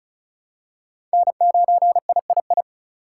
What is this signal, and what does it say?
Morse code, heard as a keyed tone, about 35 wpm, 700 Hz. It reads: N9III
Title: USA Call Signs - 5 Characters - 3x Character Spacing - 35wpm